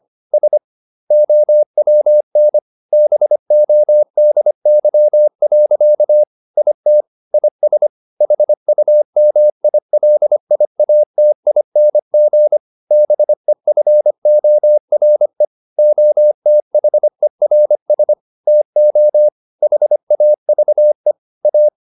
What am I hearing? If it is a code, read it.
Morse code, heard as a keyed tone, about 25 wpm, 600 Hz. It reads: S OWN BODY. IT IS HUMILIATING BEFORE OTHERS TO HAVE A
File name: the old man and the sea part 1 25wpm